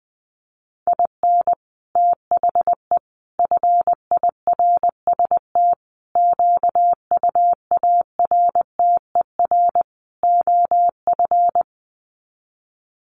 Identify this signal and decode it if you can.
Morse code, heard as a keyed tone, about 20 wpm, 700 Hz. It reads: IN THE FIRST QUARTER OF
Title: Sets of 5 Words Encore2 20wpm